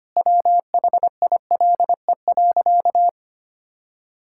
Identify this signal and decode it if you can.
Morse code, heard as a keyed tone, about 25 wpm, 700 Hz. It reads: WHILE.
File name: practice / 25wpm / Sentences with 4-Word Phrases - 25wpm